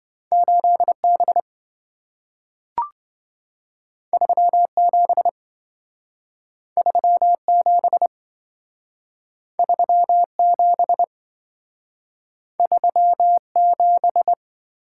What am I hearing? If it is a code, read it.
Morse code, heard as a keyed tone, about 27 wpm, 700 Hz. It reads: 8B 37 37 37 37